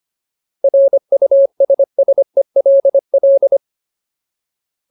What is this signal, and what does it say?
Morse code, heard as a keyed tone, about 25 wpm, 550 Hz. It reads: RUSSELL
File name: Ultimate Word List / 25WPM Ultimate